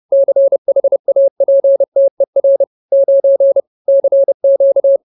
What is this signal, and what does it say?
Morse code, heard as a keyed tone, about 30 wpm, 550 Hz. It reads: CHAPTER 9 CQ